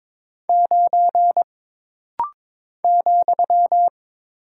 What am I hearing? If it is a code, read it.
Morse code, heard as a keyed tone, about 22 wpm, 700 Hz. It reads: 9 ,
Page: This rapid-fire timing is geared to push your ICR (Instant Character Recognition) skills.